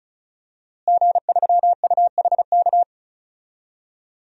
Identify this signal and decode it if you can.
Morse code, heard as a keyed tone, about 34 wpm, 700 Hz. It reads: G3UHK
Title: DXCC-100 Call Signs - 5 Characters - Repeated 3x - 35wpm